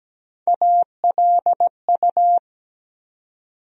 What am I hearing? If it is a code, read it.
Morse code, heard as a keyed tone, about 17 wpm, 700 Hz. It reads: ALU
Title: USA Call Sign Postfixes - 3 Letters 17wpm